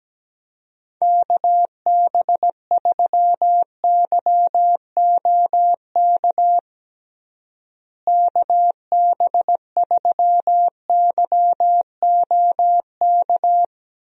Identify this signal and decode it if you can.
Morse code, heard as a keyed tone, about 17 wpm, 700 Hz. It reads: KB3YOK KB3YOK
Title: USA Call Signs - 6 Characters - Repeated 3x - 17wpm